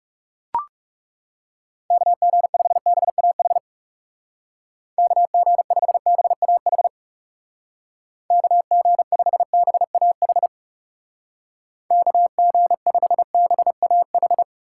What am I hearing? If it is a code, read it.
Morse code, heard as a keyed tone, about 36 wpm, 700 Hz. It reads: KG5BAH KG5BAH KG5BAH KG5BAH
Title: Speed Racing - USA 2x3 Call Signs 30wpm